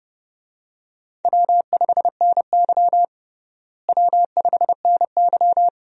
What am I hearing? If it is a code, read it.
Morse code, heard as a keyed tone, about 30 wpm, 700 Hz. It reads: W5NY W5NY
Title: USA Call Signs - 4 Characters - Repeated 3x - 30wpm